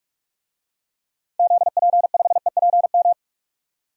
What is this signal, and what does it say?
Morse code, heard as a keyed tone, about 46 wpm, 700 Hz. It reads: ZP5EPM
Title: DXCC-100 Call Signs - 6 Characters - 45wpm